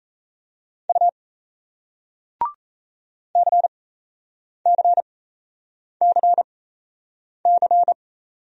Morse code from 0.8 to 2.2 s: U

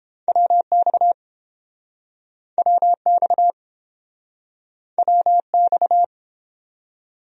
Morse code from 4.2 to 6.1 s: WX

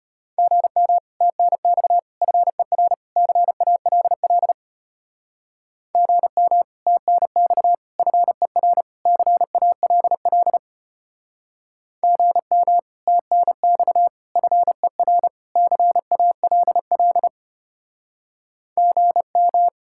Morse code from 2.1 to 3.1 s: FER

